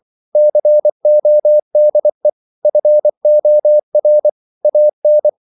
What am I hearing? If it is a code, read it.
Morse code, heard as a keyed tone, about 24 wpm, 600 Hz. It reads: CODE FOR AN